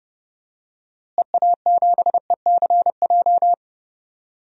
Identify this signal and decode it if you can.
Morse code, heard as a keyed tone, about 30 wpm, 700 Hz. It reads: EA7ECJ
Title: DXCC-100 Call Signs - 6 Characters - Repeated 3x - 30wpm